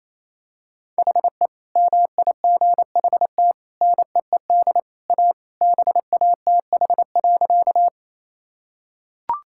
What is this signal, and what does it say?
Morse code, heard as a keyed tone, about 28 wpm, 700 Hz. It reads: HE MIGHT NEED A BATH.
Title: Sentences from Top 1600 Words 28wpm